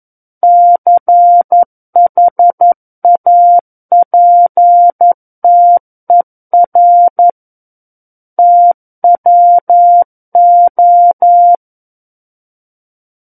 Morse code, heard as a keyed tone, about 11 wpm, 700 Hz. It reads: CHAPTER TWO
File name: 11 WPM